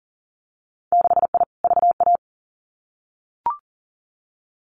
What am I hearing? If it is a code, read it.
Morse code, heard as a keyed tone, about 40 wpm, 700 Hz. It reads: 6I VA
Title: Winter Field Day Exchanges Encore 40wpm